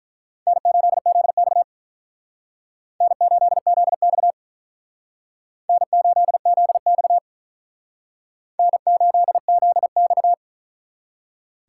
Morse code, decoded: N8ZX N8ZX N8ZX N8ZX

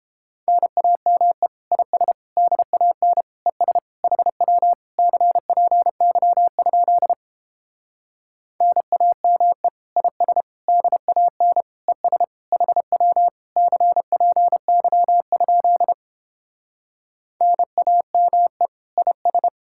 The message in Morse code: NAME IS DAN ES HW CPY? NAME IS DAN ES HW CPY? NAME IS

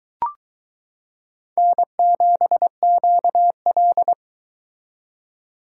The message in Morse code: N7QL

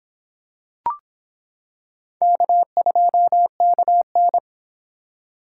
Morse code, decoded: K2KN

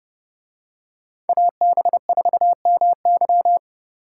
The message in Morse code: AB4MY

